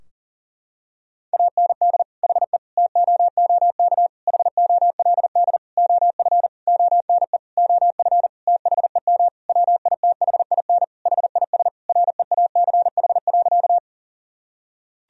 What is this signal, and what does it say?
Morse code, heard as a keyed tone, about 40 wpm, 700 Hz. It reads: AND HE TOOK HOLD OF ONE OF THEM WITHIN HIS REACH.
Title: Sentences from Top 1000 Words - Review Part 4 - 40wpm